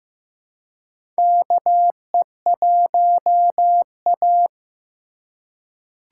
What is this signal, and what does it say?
Morse code, heard as a keyed tone, about 15 wpm, 700 Hz. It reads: KE1A
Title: USA Call Signs - 4 Characters - 4x Character Spacing - 15wpm